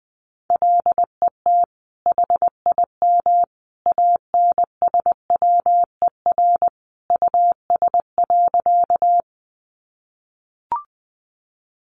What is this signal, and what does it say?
Morse code, heard as a keyed tone, about 20 wpm, 700 Hz. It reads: LET HIM ANSWER US.
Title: Sentences from Top 300 Words 20wpm